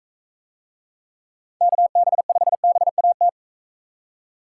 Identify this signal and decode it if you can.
Morse code, heard as a keyed tone, about 43 wpm, 700 Hz. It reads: KB5BAT